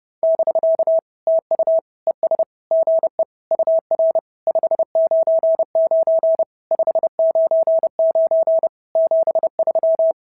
Morse code, heard as a keyed tone, about 30 wpm, 650 Hz. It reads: <BK> TU ES GE UR 599 599 73
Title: SOTA QSO Practice 30wpm